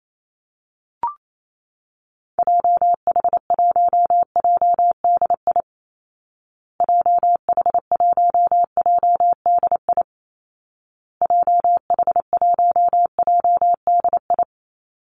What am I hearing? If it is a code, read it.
Morse code, heard as a keyed tone, about 28 wpm, 700 Hz. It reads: JH1JDI JH1JDI JH1JDI
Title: Call Signs Repeated 2x - 28wpm